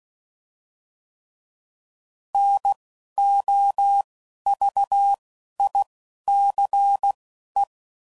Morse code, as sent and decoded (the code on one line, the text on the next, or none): -. / --- / ...- / .. / -.-. / .
N O V I C E